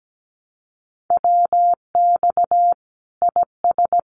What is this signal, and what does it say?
Morse code, heard as a keyed tone, about 17 wpm, 700 Hz. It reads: WX IS
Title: Common QSO Phrases 17wpm